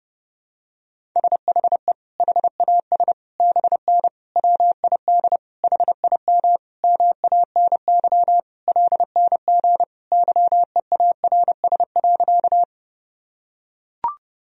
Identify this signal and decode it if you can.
Morse code, heard as a keyed tone, about 30 wpm, 700 Hz. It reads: SHE HAS BN WID HIM MANY LNG YEARS.
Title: Sentences from Top 200 Words and Abbreviations 30wpm